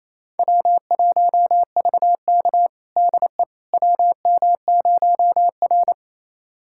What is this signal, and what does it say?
Morse code, heard as a keyed tone, about 28 wpm, 700 Hz. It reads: W1VK DE WM0R
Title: USA Callsign DE Callsign - 4 Characters - 28wpm